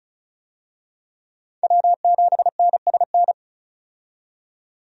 Morse code, heard as a keyed tone, about 36 wpm, 700 Hz. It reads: W7NSN
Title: USA Call Signs - 5 Characters - 35wpm